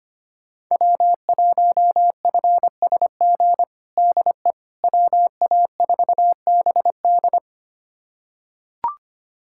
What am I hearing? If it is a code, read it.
Morse code, heard as a keyed tone, about 25 wpm, 700 Hz. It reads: W1FSG DE WA4BD
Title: USA Callsign DE Callsign - 5 Characters - 25wpm